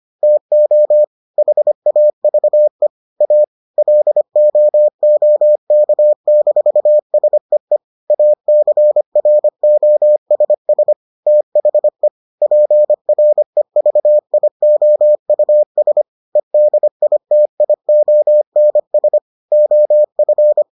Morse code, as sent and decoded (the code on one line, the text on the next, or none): - --- / .... .- ...- . / .- / .-.. --- --- -.- -....- ... . / .- -.-. .-. --- ... ... / - .... / .--. .-. ...- .. --- ..- ... / . -.. .. - .. --- -. ... / --- ..-.
TO HAVE A LOOK-SEE ACROSS THE PREVIOUS EDITIONS OF